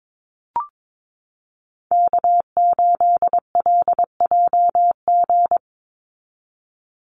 Morse code, heard as a keyed tone, about 22 wpm, 700 Hz. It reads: K8LJG
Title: NA SOTA Chaser Callsigns 22wpm